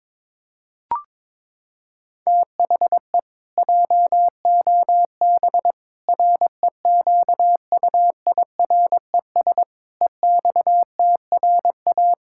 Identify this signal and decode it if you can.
Morse code, heard as a keyed tone, about 22 wpm, 700 Hz. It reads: THE JOB REQUIRES EXTRA
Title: Pangram Challenge v2 - 22wpm